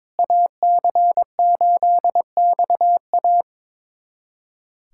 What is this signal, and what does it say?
Morse code, heard as a keyed tone, about 22 wpm, 700 Hz. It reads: AC8XA
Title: USA Callsign DE Callsign - 5 Characters - 22wpm